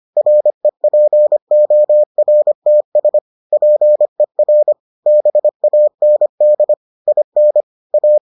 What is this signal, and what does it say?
Morse code, heard as a keyed tone, about 25 wpm, 600 Hz. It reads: REPORTS PER BAND IN A